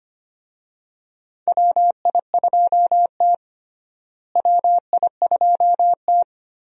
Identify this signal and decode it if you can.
Morse code, heard as a keyed tone, about 25 wpm, 700 Hz. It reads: WI2T WI2T
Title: USA Call Signs - 4 Characters - Repeated 3x - 25wpm